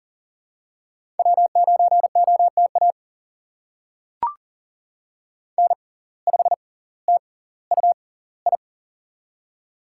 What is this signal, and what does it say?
Morse code, heard as a keyed tone, about 39 wpm, spaced as for 20 wpm, 700 Hz. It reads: W9OTA N5TUI
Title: USA Call Signs - 5 Characters - 6x Character Spacing - 40wpm